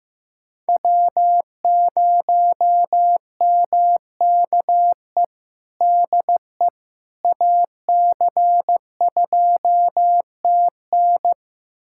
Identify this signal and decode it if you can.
Morse code, heard as a keyed tone, about 15 wpm, 700 Hz. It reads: W0MKE DE AC2TN
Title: USA Callsign DE Callsign - 5 Characters - 15wpm